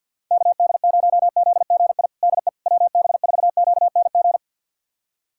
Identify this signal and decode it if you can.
Morse code, heard as a keyed tone, about 51 wpm, 700 Hz. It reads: KD0ZGI DE WB4QNG
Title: USA Callsign DE Callsign - 6 Characters - 50wpm